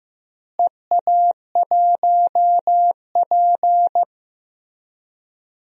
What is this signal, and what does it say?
Morse code, heard as a keyed tone, about 16 wpm, 700 Hz. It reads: EA1P
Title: DXCC-100 Call Signs - 4 Characters - Repeated 3x - 15wpm